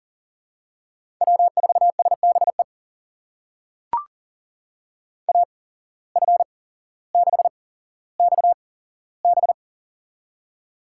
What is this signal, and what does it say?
Morse code, heard as a keyed tone, about 37 wpm, spaced as for 17 wpm, 700 Hz. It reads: W4SBE AF6XB
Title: USA Call Signs - 5 Characters - 8x Character Spacing - 40wpm